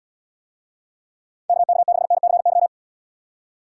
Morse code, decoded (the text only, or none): KK6NCY